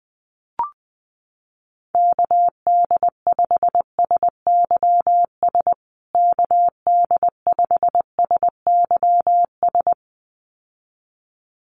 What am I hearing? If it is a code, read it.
Morse code, heard as a keyed tone, about 19 wpm, 700 Hz. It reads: KD5SYS KD5SYS
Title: USA Call Signs Repeated 2x 20wpm